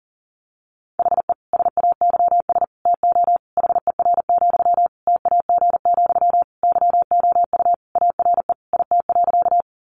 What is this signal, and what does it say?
Morse code, heard as a keyed tone, about 40 wpm, 700 Hz. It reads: HE SAYS TO HER, TAG, YOU ARE IT.
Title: Additional Easy Phrases 40wpm